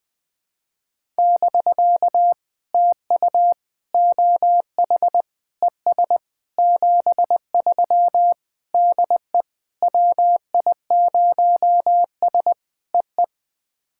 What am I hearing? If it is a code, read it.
Morse code, heard as a keyed tone, about 20 wpm, 700 Hz. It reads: <BK> TU OH ES 73 DE WI0S EE
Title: POTA QSO Practice - 20wpm